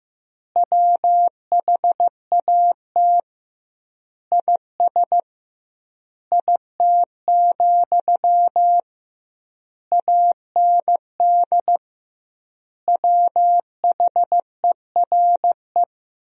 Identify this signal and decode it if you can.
Morse code, heard as a keyed tone, about 15 wpm, 700 Hz. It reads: WHAT IS IT, AND WHERE
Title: 2x Word-Spacing - Sentences from Top 300 Words - 15wpm